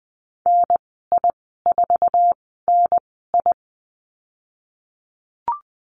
Morse code, heard as a keyed tone, about 20 wpm, spaced as for 15 wpm, 700 Hz. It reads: NI4NI
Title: 048 Lesson 9g - Number 4 - Call Signs 20wpm